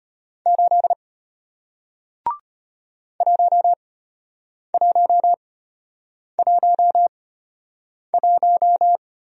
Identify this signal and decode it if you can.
Morse code, heard as a keyed tone, about 33 wpm, spaced as for 8 wpm, 700 Hz. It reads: 8 1111